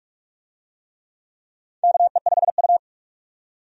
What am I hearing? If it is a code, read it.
Morse code, heard as a keyed tone, about 46 wpm, 700 Hz. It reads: KE5U
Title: USA Call Signs - 4 Characters - 45wpm